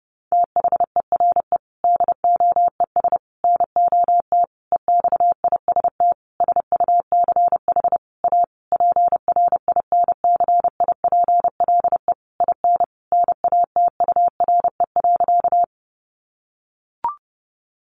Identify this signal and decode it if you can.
Morse code, heard as a keyed tone, about 28 wpm, 700 Hz. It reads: THERE DOES NOT EXIST SUCH A PRINCIPLE IN NATURE.
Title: Sentences from Top 1400 Words 30wpm